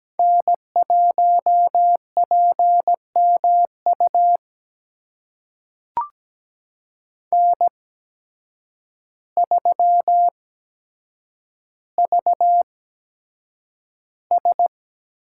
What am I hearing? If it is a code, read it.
Morse code, heard as a keyed tone, about 17 wpm, spaced as for 8 wpm, 700 Hz. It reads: N1PMU N3VS
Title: USA Call Signs - 5 Characters - 8x Character Spacing - 17wpm